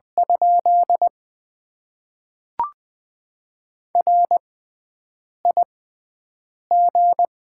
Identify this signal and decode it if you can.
Morse code, heard as a keyed tone, about 20 wpm, spaced as for 7 wpm, 690 Hz. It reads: ? RIG
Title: Top 20 QSO Elements - 6x Character Spacing - 20wpm